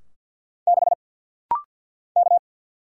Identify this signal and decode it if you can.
Morse code, heard as a keyed tone, about 50 wpm, 700 Hz. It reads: 6 K
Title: Single Letter-Number - Rapid-Fire 50wpm